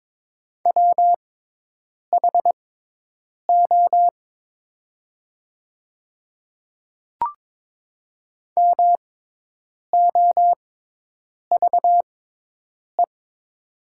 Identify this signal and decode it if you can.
Morse code, heard as a keyed tone, about 22 wpm, spaced as for 8 wpm, 700 Hz. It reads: WHO MOVE
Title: 6x Character Spacing - Top 200 Words - 22wpm